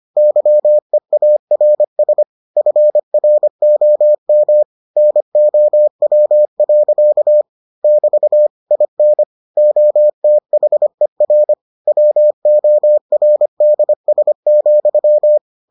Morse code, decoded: YEARS FROM NOW. <BT> IN OTHER WORDS,

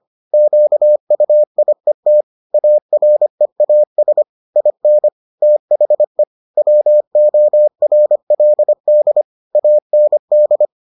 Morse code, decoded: QUIET AREAS IN THE WORLD AND

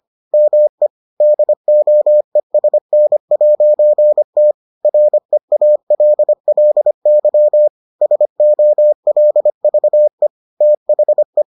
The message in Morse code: ME DOESN'T REALLY SOLVE THE

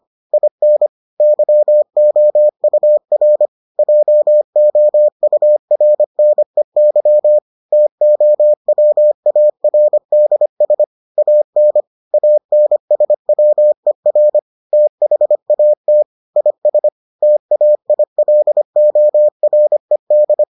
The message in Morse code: IN YOUR JOURNEY TOWARDS AN ANSWER THAT IS TAILORED

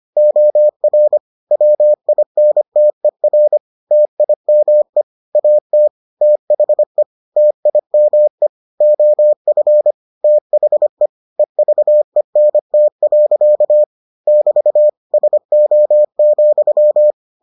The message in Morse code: OR WINTER TIME AT THE TIME OF THE EVENT. <BT> SO,

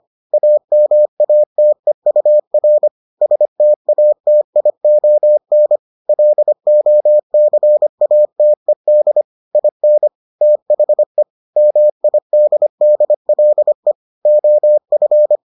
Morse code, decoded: AMATEUR STATION LOCATED IN THE MIDDLE OF

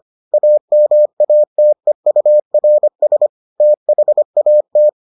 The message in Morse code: AMATEURS THAT

